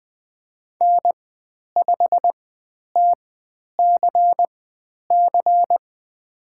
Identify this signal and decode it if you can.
Morse code, heard as a keyed tone, about 20 wpm, spaced as for 10 wpm, 700 Hz. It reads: N5TCC
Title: 102 Lesson 17g - Letter u - Call Signs 20wpm